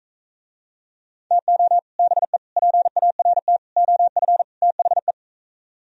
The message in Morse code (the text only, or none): TO BE PART OF THE